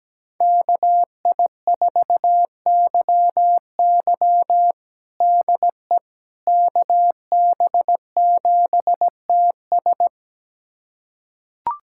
Morse code, heard as a keyed tone, about 17 wpm, 700 Hz. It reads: KI4YY DE KB7TS
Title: USA Callsign DE Callsign - 5 Characters - 17wpm